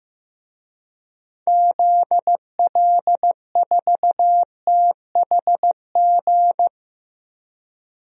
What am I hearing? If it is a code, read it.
Morse code, heard as a keyed tone, about 15 wpm, 700 Hz. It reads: ZL4THG
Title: DXCC-100 Call Signs - 6 Characters - 15wpm